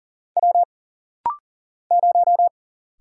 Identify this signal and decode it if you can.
Morse code, heard as a keyed tone, about 40 wpm, 700 Hz. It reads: W 0
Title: Single Letter-Number Alt - Rapid-Fire 40wpm